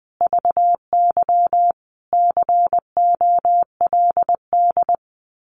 Morse code, heard as a keyed tone, about 20 wpm, 700 Hz. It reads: VY COLD
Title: Common QSO Phrases 20wpm